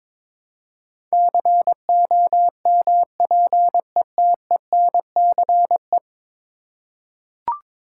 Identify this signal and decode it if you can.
Morse code, heard as a keyed tone, about 21 wpm, 700 Hz. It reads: COMPETENCE
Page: morse-code-ninja-practice-22wpm